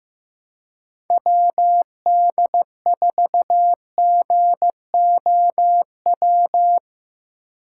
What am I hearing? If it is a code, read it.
Morse code, heard as a keyed tone, about 15 wpm, 700 Hz. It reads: WD4GOW
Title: USA Call Signs - 6 Characters - 6x Character Spacing - 15wpm